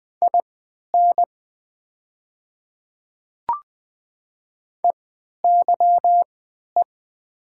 Morse code, heard as a keyed tone, about 21 wpm, spaced as for 11 wpm, 690 Hz. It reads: IN EYE
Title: 3x Character Spacing - Top 300 Words - 20wpm